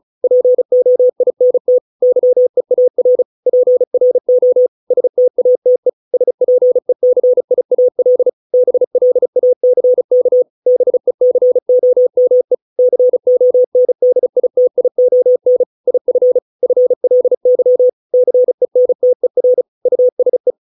Morse code, heard as a keyed tone, about 34 wpm, 500 Hz. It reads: POINT YEAR PRO STATE SPECIAL BLACK BECOME CONDITION IF FLY CENTER USE